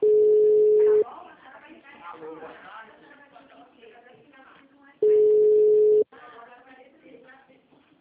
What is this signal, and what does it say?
Ringback tone (Europe, 425 Hz), with speech.